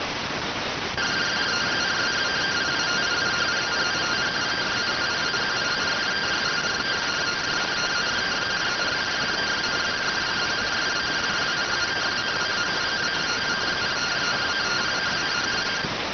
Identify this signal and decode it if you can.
Two-tone teleprinter signal, about 170 Hz shift.